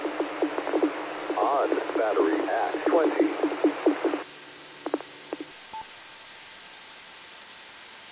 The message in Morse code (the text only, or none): none